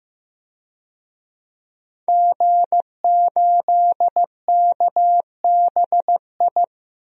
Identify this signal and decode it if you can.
Morse code, heard as a keyed tone, about 15 wpm, 700 Hz. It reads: G8KBI
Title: DXCC-100 Call Signs - 5 Characters - 15wpm